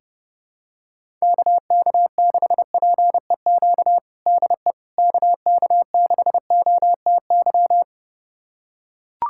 Morse code, heard as a keyed tone, about 30 wpm, 700 Hz. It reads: KK6PEQ DE KK6OTY
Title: USA Callsign DE Callsign - 6 Characters - 30wpm